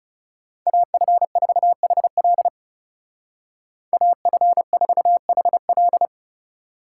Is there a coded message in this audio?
Yes: Morse code from 0.7 to 6.1 s, heard as a keyed tone, about 39 wpm, 700 Hz. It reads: AF4HL AF4HL